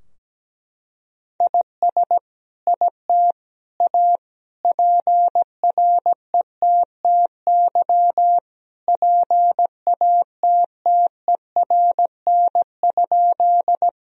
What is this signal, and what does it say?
Morse code, heard as a keyed tone, about 17 wpm, 700 Hz. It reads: IS IT A PRETTY PATTERN?
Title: Sentences from Top 1000 Words - Review Part 2 - 17wpm